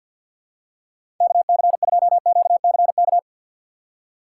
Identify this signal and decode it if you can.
Morse code, heard as a keyed tone, about 50 wpm, 700 Hz. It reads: KC2QCK
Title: USA Call Signs - 6 Characters - 4x Character Spacing - 50wpm